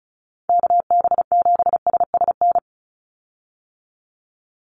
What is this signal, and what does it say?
Morse code, heard as a keyed tone, about 34 wpm, 700 Hz. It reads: KB7SSN
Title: USA Call Signs 35wpm